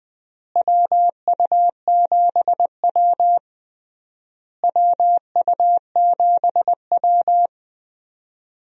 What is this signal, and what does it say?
Morse code, heard as a keyed tone, about 20 wpm, 700 Hz. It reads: WU7W WU7W
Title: USA Call Signs - 4 Characters - Repeated 3x - 20wpm